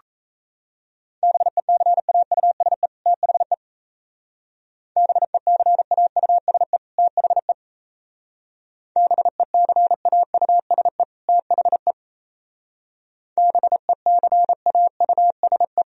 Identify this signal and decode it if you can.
Morse code, heard as a keyed tone, about 32 wpm, 700 Hz. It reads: BECAUSE THE BECAUSE THE BECAUSE THE BECAUSE
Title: Speed-Racing - Sets of 2 Words 28wpm